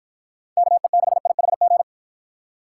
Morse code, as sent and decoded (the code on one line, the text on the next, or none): -.- . -.... .. .... --.
KE6IHG